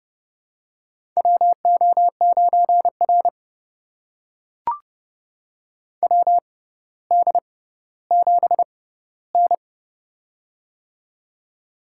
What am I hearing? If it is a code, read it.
Morse code, heard as a keyed tone, about 30 wpm, spaced as for 15 wpm, 700 Hz. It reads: WO9R WD7N
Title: USA Call Signs - 4 Characters - 6x Character Spacing - 30wpm